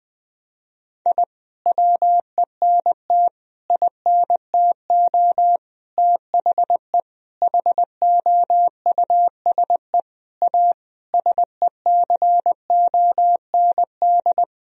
Morse code, heard as a keyed tone, about 20 wpm, 700 Hz. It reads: I WENT INTO THE HOUSE A SECOND
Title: Sentences from Top 1000 Words - Review Part 1 - 20wpm